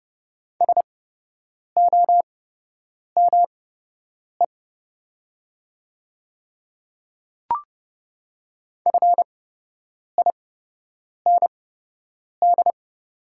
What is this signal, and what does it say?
Morse code, heard as a keyed tone, about 30 wpm, spaced as for 8 wpm, 700 Hz. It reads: SOME FIND
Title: 8x Character Spacing - Top 200 Words - 30wpm